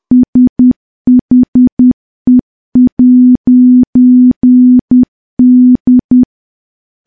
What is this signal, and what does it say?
Morse code, heard as a keyed tone, about 10 wpm, 250 Hz. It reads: SHE'D
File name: morse / morse-training-10WPM